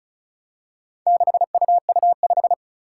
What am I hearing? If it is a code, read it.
Morse code, heard as a keyed tone, about 36 wpm, 700 Hz. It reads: 6UU5
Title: Four Challenging Letter-Number Combinations 35wpm